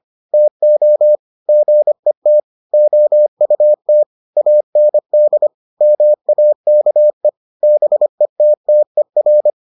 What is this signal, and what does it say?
Morse code, heard as a keyed tone, about 25 wpm, 600 Hz. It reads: TO GET OUT AND MAKE BETTER